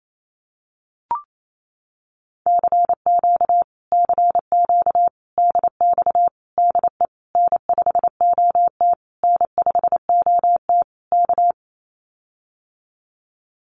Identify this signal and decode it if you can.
Morse code, heard as a keyed tone, about 28 wpm, 700 Hz. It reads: CQ CQ DX DE N5OT N5OT K
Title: Common QSO Phrases Encore 28wpm